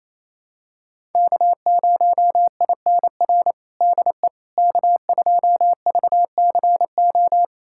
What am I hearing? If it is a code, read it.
Morse code, heard as a keyed tone, about 28 wpm, 700 Hz. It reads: K0INR DE K2VCO
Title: USA Callsign DE Callsign - 5 Characters - 28wpm